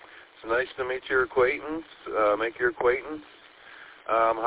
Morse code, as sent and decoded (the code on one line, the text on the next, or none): none